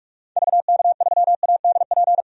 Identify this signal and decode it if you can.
Morse code, heard as a keyed tone, about 46 wpm, 700 Hz. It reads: VK3ADP
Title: DXCC-100 Call Signs - 6 Characters - 45wpm